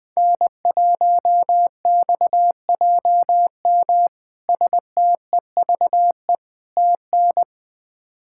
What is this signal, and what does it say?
Morse code, heard as a keyed tone, about 20 wpm, 700 Hz. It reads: N1XJM STEVE TN
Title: K1USN SST QSO Practice - 20wpm